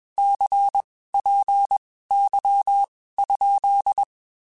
Morse code, heard as a keyed tone, about 21 wpm, spaced as for 15 wpm, 800 Hz. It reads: CPY?
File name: (spaced) / QSOs Spaced-mp3s / WD0FBT @20 sp